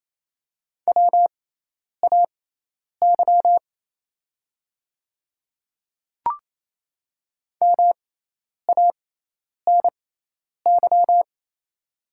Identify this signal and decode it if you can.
Morse code, heard as a keyed tone, about 28 wpm, spaced as for 10 wpm, 700 Hz. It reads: WAY MANY
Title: 6x Character Spacing - Top 100 Words - 28wpm